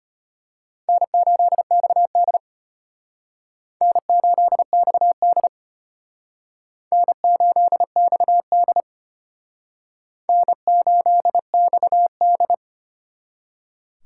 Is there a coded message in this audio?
Yes: Morse code from 0.9 to 12.5 s, heard as a keyed tone, about 34 wpm, 700 Hz. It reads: N8XD N8XD N8XD N8XD